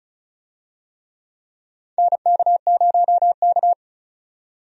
Morse code, heard as a keyed tone, about 36 wpm, 700 Hz. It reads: NK0K
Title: USA Call Signs - 4 Characters - 35wpm